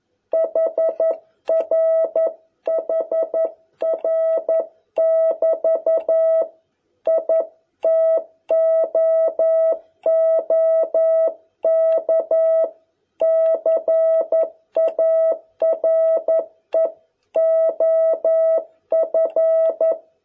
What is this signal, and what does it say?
Morse code, heard as a keyed tone, about 11 wpm, 650 Hz. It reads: HR HR <BT> I TOOK CARE OF